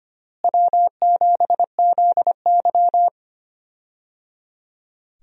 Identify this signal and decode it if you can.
Morse code, heard as a keyed tone, about 25 wpm, 700 Hz. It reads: W7ZY
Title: USA Call Signs - 4 Characters - Repeated 3x - 25wpm